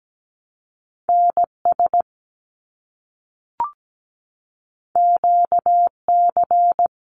Morse code, heard as a keyed tone, about 17 wpm, 700 Hz. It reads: NS QC
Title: Canadian Provinces and Territories - Repeated 3x - 17wpm